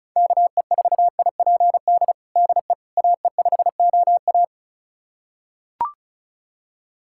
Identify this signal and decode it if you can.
Morse code, heard as a keyed tone, about 36 wpm, 700 Hz. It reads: KE4IPD DE AE5OA
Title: USA Callsign DE Callsign 35wpm